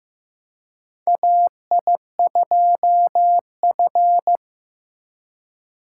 Morse code, heard as a keyed tone, about 15 wpm, 700 Hz. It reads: AI2F
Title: USA Call Signs - 4 Characters - 3x Character Spacing - 15wpm